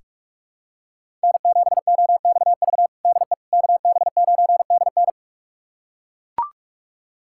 Morse code, heard as a keyed tone, about 41 wpm, 700 Hz. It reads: N7OXV DE KB9DN
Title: USA Callsign DE Callsign - 5 Characters - 45wpm